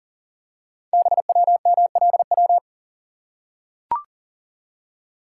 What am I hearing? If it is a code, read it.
Morse code, heard as a keyed tone, about 39 wpm, 690 Hz. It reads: BWMLW
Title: Code Groups - 5 Letters - 40wpm